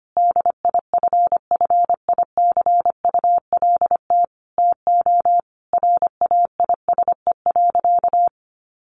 Morse code, heard as a keyed tone, about 25 wpm, 700 Hz. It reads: DIFFICULT TO RAISE.
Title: Sentences from Top 1400 Words 25wpm